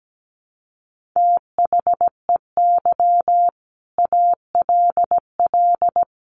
Morse code, heard as a keyed tone, about 17 wpm, 700 Hz. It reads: THEY ALL
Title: Sentences from Top 400 Words 17wpm